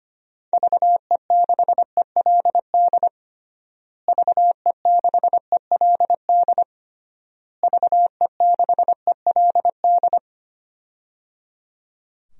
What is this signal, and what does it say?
Morse code, heard as a keyed tone, about 25 wpm, 700 Hz. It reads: VE6ELD VE6ELD VE6ELD
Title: DXCC-100 Call Signs - 6 Characters - Repeated 3x - 25wpm